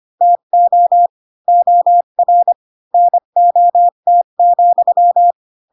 Morse code, heard as a keyed tone, about 25 wpm, 700 Hz. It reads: TO OR NOT,
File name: Pinocchio / Pinocchio ch4 20 25 ik1hge